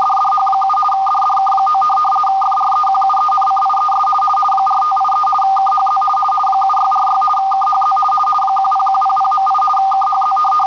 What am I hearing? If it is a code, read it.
RTTY, about 50 baud. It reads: DDH7 DDK9 FREQUENCIES 4583 KHZ 7646 KHZ 10100.8 KHZ